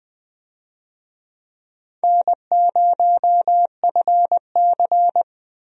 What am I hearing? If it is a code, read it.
Morse code, heard as a keyed tone, about 20 wpm, 700 Hz. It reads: N0FC
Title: USA Call Signs - 4 Characters - Repeated 2x - 20wpm